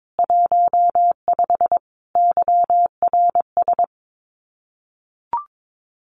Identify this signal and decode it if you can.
Morse code, heard as a keyed tone, about 22 wpm, 700 Hz. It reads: 15 YRS
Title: Common QSO Phrases Encore 22wpm